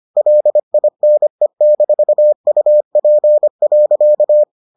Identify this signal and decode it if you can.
Morse code, heard as a keyed tone, about 25 wpm, 600 Hz. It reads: LINE-UP.